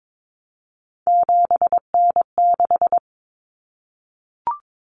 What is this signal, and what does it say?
Morse code, heard as a keyed tone, about 22 wpm, 700 Hz. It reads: 7N6
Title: Common Cut Numbers - 3-Digit 22wpm